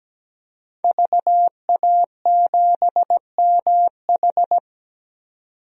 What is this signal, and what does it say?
Morse code, heard as a keyed tone, about 17 wpm, 700 Hz. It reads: VA7MH
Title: DXCC-100 Call Signs - 5 Characters - 17wpm